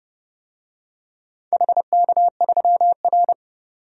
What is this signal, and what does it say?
Morse code, heard as a keyed tone, about 30 wpm, 700 Hz. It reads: HK3R
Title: DXCC-100 Call Signs - 4 Characters - 30wpm